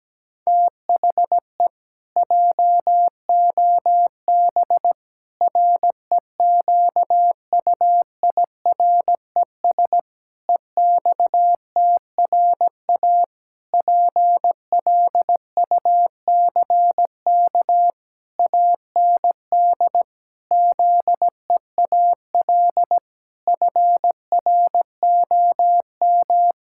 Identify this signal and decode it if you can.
Morse code, heard as a keyed tone, about 17 wpm, 700 Hz. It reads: THE JOB REQUIRES EXTRA PLUCK AND ZEAL FROM
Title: Pangram Challenge v2 - 17wpm